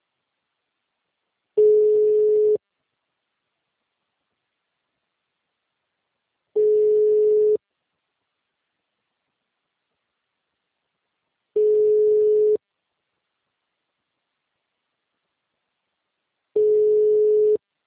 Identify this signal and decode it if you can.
Ringback tone (Europe, 425 Hz).